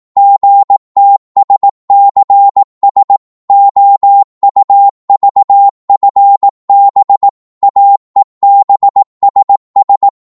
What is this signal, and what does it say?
Morse code, heard as a keyed tone, about 18 wpm, 800 Hz. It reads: GTSCS OUVFB AEBSS
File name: Morse / ltr-18wpm